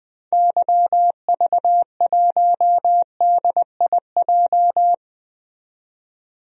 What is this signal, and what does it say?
Morse code, heard as a keyed tone, about 20 wpm, 700 Hz. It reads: YV1DIJ
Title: DXCC-100 Call Signs - 6 Characters - 20wpm